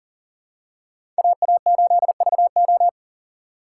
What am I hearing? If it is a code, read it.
Morse code, heard as a keyed tone, about 40 wpm, 700 Hz. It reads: AA8VO
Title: USA Call Signs - 5 Characters - 2x Character Spacing - 40wpm